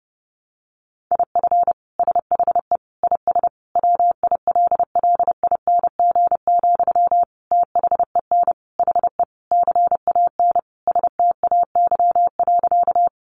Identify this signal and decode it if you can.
Morse code, heard as a keyed tone, about 30 wpm, 700 Hz. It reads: IF SHE IS WILLING, THEN HE CAN STAY.
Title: Sentences from Top 1000 Words 30wpm